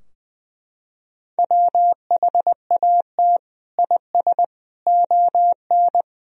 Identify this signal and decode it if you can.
Morse code, heard as a keyed tone, about 20 wpm, 700 Hz. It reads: WHAT IS ON
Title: Sentences from Top 1000 Words - Review Part 4 - 20wpm